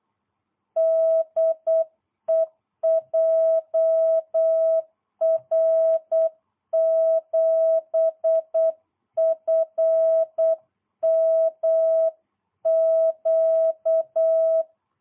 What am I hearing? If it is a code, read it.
Morse code, heard as a keyed tone, about 8 wpm, 660 Hz. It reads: DE JR7FMQ